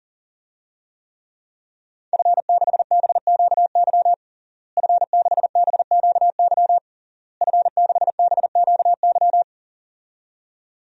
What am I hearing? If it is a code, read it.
Morse code, heard as a keyed tone, about 40 wpm, 700 Hz. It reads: F6BQY F6BQY F6BQY
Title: DXCC-100 Call Signs - 5 Characters - Repeated 3x - 40wpm